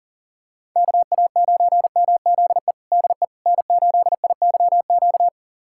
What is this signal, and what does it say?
Morse code, heard as a keyed tone, about 40 wpm, 700 Hz. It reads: KA9MZE DE N8IYQ
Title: USA Callsign DE Callsign 40wpm